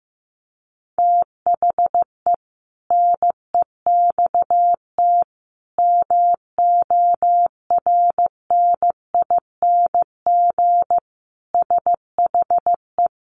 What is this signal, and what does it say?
Morse code, heard as a keyed tone, about 15 wpm, 700 Hz. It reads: THE NEXT MORNING SHE
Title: Sentences from Top 1200 Words 15wpm